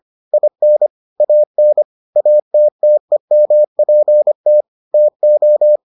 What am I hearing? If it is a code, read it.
Morse code, heard as a keyed tone, about 25 wpm, 600 Hz. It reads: IN AN ATTEMPT TO